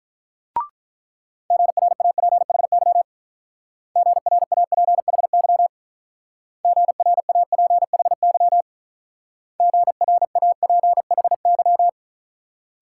Morse code, heard as a keyed tone, about 44 wpm, 700 Hz. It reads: GRAPHY GRAPHY GRAPHY GRAPHY